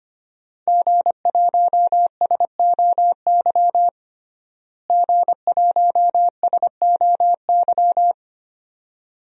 Morse code, heard as a keyed tone, about 25 wpm, 700 Hz. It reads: G1SOY G1SOY
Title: DXCC-100 Call Signs - 5 Characters - Repeated 2x - 25wpm